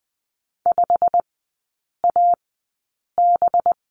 Morse code, heard as a keyed tone, about 20 wpm, spaced as for 8 wpm, 700 Hz. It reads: 5AB
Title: 184 Lesson 30e - Letter b - Three Characters 20wpm